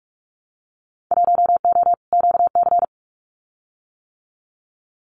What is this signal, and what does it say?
Morse code, heard as a keyed tone, about 44 wpm, 700 Hz. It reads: 1O QC
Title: Winter Field Day Exchanges 45wpm